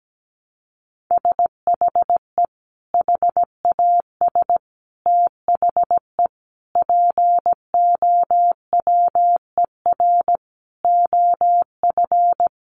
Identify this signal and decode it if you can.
Morse code, heard as a keyed tone, about 17 wpm, 700 Hz. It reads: SHE HAS THE POWER OF
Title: Sentences from Top 1200 Words 17wpm